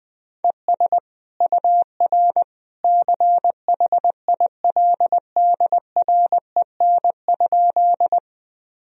Morse code, heard as a keyed tone, about 20 wpm, 700 Hz. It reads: ES UR CHILDREN?
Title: Sentences from Top 300 Words and Abbreviations 20wpm